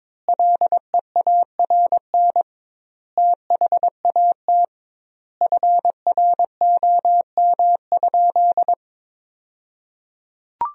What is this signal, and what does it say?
Morse code, heard as a keyed tone, about 22 wpm, 700 Hz. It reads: LEARN THAT FROM?
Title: 2x Word-Spacing - Sentences from Top 300 Words - 22wpm